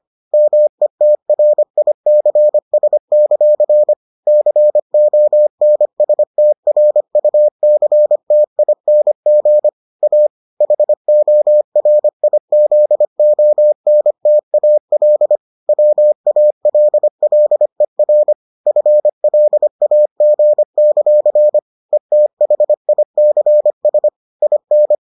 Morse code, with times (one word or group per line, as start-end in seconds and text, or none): 0.3-3.9 METRICS;
4.3-9.7 CONSTRUCTING
10.0-10.3 A
10.6-15.4 HORIZONTAL
15.7-18.3 WALLER
18.7-21.6 FLAG;
21.9-24.1 ETHICS
24.4-24.9 IN